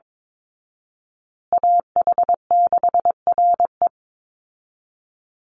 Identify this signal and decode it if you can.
Morse code, heard as a keyed tone, about 22 wpm, 700 Hz. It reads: AH6RE
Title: Call Signs Repeated 3x - 22wpm